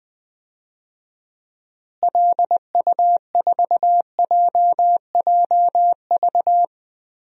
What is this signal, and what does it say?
Morse code, heard as a keyed tone, about 20 wpm, 700 Hz. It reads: LU4JJV
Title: DXCC-100 Call Signs - 6 Characters - Repeated 3x - 20wpm